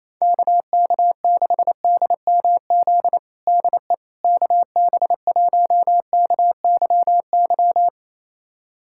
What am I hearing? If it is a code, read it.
Morse code, heard as a keyed tone, about 28 wpm, 700 Hz. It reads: KK6DMZ DE KB1KYY
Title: USA Callsign DE Callsign - 6 Characters - 28wpm